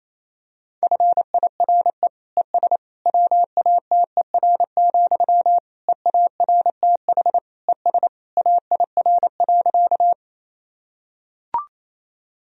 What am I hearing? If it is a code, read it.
Morse code, heard as a keyed tone, about 28 wpm, 700 Hz. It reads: FIRE ES WATER, EARTH ES AIR.
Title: Sentences from Top 400 Words and Abbreviations 28wpm